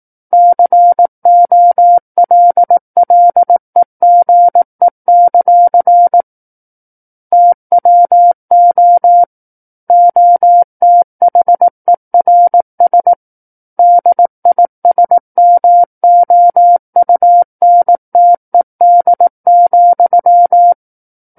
Morse code, heard as a keyed tone, about 18 wpm, 700 Hz. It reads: COLLEGE; TWO OTHERS DISMOUNTED,
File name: 18 WPM